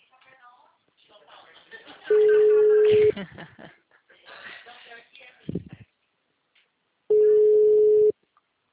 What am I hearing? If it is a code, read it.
Ringback tone (Europe, 425 Hz), with speech.
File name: out- -2013- - - .12274